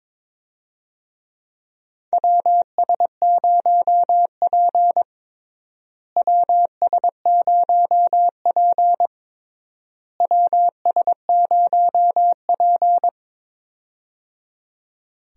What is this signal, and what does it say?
Morse code, heard as a keyed tone, about 21 wpm, 700 Hz. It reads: WS0P WS0P WS0P
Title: USA Call Signs - 4 Characters - Repeated 3x - 22wpm